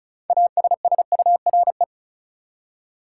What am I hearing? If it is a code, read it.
Morse code, heard as a keyed tone, about 36 wpm, 700 Hz. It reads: ASSURE
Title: Six Letter Words - 35wpm